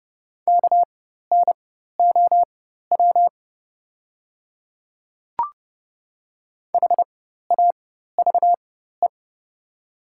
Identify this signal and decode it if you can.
Morse code, heard as a keyed tone, about 30 wpm, spaced as for 14 wpm, 700 Hz. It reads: KNOW HAVE
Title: 4x Character Spacing - Top 200 Words - 30wpm